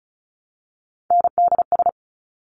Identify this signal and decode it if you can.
Morse code, heard as a keyed tone, about 36 wpm, 700 Hz. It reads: NDS
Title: Top Three Letter Combinations in Words 35wpm